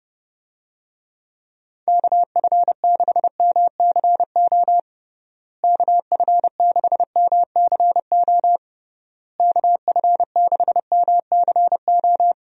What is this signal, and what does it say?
Morse code, heard as a keyed tone, about 30 wpm, 700 Hz. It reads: KF6MCO KF6MCO KF6MCO
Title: USA Call Signs - 6 Characters - Repeated 3x - 30wpm